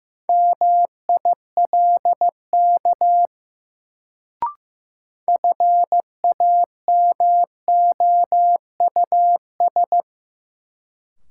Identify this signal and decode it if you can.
Morse code, heard as a keyed tone, about 15 wpm, 700 Hz. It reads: MILK FAMOUS
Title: Sentences from Top 1500 Words - New Word Drill Alt - 15wpm